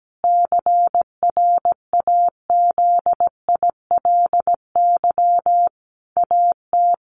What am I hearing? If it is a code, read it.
Morse code, heard as a keyed tone, about 17 wpm, 700 Hz. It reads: CRAZILY AT
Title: A Sound Of Thunder 17wpm